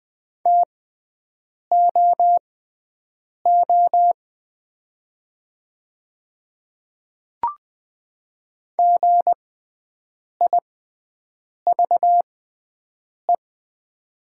Morse code, heard as a keyed tone, about 20 wpm, spaced as for 7 wpm, 700 Hz. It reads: TOO GIVE
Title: 6x Character Spacing - Top 200 Words - 20wpm